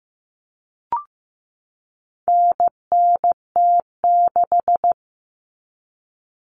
Morse code, heard as a keyed tone, about 15 wpm, 700 Hz. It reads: NNT6
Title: Common Cut Numbers - 4-Digit 15wpm